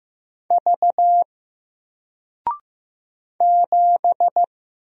Morse code, heard as a keyed tone, about 15 wpm, 700 Hz. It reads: V 7
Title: Single Letter-Number Alt 15wpm